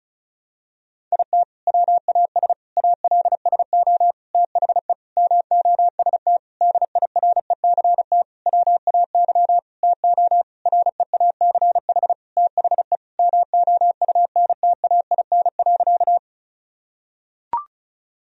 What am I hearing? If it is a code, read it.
Morse code, heard as a keyed tone, about 34 wpm, 700 Hz. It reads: IT WAS ALSO THE MOST DIRECT WAY TO REACH THE MOUNTAIN.
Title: Sentences from Top 1000 Words - Review Part 2 - 35wpm